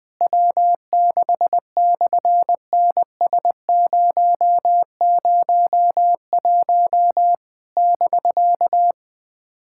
Morse code, heard as a keyed tone, about 19 wpm, 700 Hz. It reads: W6/NS001 <BK>
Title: SOTA QSO Practice 20wpm